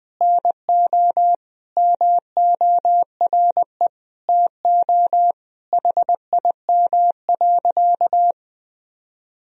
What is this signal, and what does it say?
Morse code, heard as a keyed tone, about 20 wpm, 700 Hz. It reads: NO MORE TO HIM.
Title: Sentences from Top 1000 Words - Review Part 1 - 20wpm